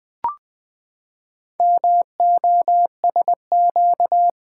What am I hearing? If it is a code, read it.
Morse code, heard as a keyed tone, about 20 wpm, 700 Hz. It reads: MOSQ
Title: Build A Word - 4 Letters At A Time - 8-Letter Words - 20wpm